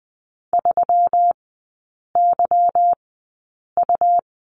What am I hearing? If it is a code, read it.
Morse code, heard as a keyed tone, about 20 wpm, spaced as for 8 wpm, 700 Hz. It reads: 3YU
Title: 152 Lesson 25e - Letter p - Three Characters 20wpm